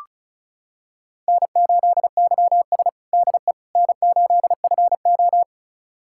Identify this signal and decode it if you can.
Morse code, heard as a keyed tone, about 34 wpm, 700 Hz. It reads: N8YS DE N8FO
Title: USA Callsign DE Callsign - 4 Characters - 35wpm